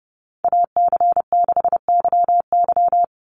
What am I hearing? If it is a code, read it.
Morse code, heard as a keyed tone, about 30 wpm, 700 Hz. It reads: AC6YY
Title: Call Signs Repeated 2x - 30wpm